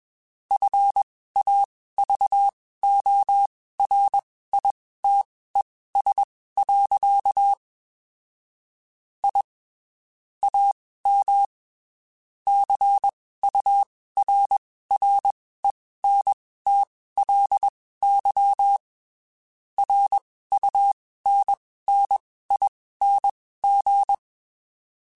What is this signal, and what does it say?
Morse code, heard as a keyed tone, about 21 wpm, spaced as for 15 wpm, 800 Hz. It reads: FAVORITES. I AM CURRENTLY RUNNING